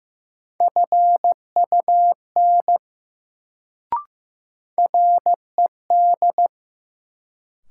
Morse code, heard as a keyed tone, about 15 wpm, 700 Hz. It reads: FUN RED
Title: Sentences from Top 700 Words - New Word Drill - 15wpm